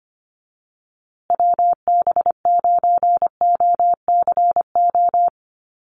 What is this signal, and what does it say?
Morse code, heard as a keyed tone, about 25 wpm, 700 Hz. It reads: WB9OCO
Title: USA Call Signs Repeated 3x 25wpm